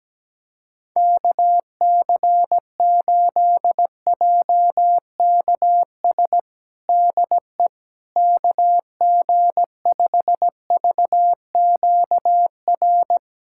Morse code, heard as a keyed tone, about 17 wpm, 700 Hz. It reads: KC8JKS DE KG5VQR
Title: USA Callsign DE Callsign - 6 Characters - 17wpm